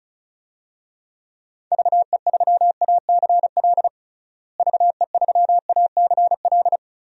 Morse code, heard as a keyed tone, about 36 wpm, 700 Hz. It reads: VE3ACL VE3ACL
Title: DXCC-100 Call Signs - 6 Characters - Repeated 3x - 35wpm